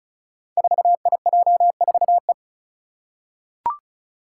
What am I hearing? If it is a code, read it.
Morse code, heard as a keyed tone, about 36 wpm, 700 Hz. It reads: 4IJ4E
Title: Five Challenging Letter-Number Combinations 35wpm